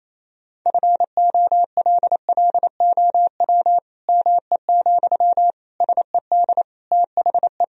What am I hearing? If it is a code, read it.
Morse code, heard as a keyed tone, about 27 wpm, 700 Hz. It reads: FOLLOW ME, SED THE
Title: Sentences from Top 200 Words and Abbreviations 28wpm